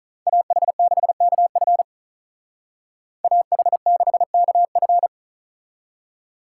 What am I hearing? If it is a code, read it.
Morse code, heard as a keyed tone, about 43 wpm, 700 Hz. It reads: AH6KF AH6KF
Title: Speed Racing - USA 2x2 Call Signs 35wpm